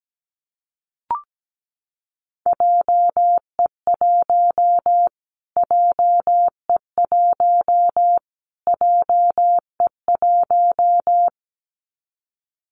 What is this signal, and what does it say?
Morse code, heard as a keyed tone, about 17 wpm, 700 Hz. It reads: JE1 JE1 JE1
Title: Callsign Prefixes for Top 25 DXCC - Repeated 3x - 17wpm